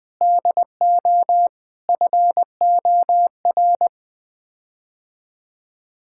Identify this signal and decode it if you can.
Morse code, heard as a keyed tone, about 20 wpm, 700 Hz. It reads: DO FOR
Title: Sets of 8 Words - 20wpm